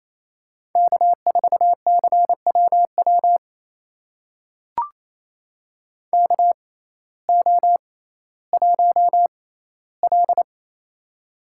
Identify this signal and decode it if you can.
Morse code, heard as a keyed tone, about 28 wpm, spaced as for 15 wpm, 700 Hz. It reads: K4CWW KO1L
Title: USA Call Signs - 6x Character Spacing - 28wpm